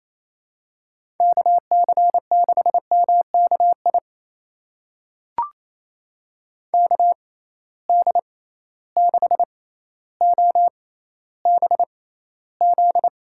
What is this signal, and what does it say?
Morse code, heard as a keyed tone, about 28 wpm, spaced as for 14 wpm, 700 Hz. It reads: KC6MKI KD6OBZ
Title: USA Call Signs - 6x Character Spacing - 28wpm